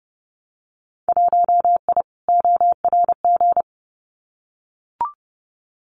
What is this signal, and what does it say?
Morse code, heard as a keyed tone, about 30 wpm, 700 Hz. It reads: 1I ORG
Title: Winter Field Day Exchanges 30wpm